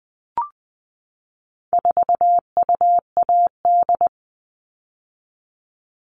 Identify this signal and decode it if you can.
Morse code, heard as a keyed tone, about 20 wpm, 700 Hz. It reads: 4UAD
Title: All Cut Numbers - 4-Digit 20wpm